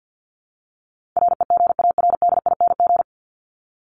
Morse code, heard as a keyed tone, about 50 wpm, 700 Hz. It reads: REGARDING
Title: ING Suffix - Top 100 Words with Base Words 50wpm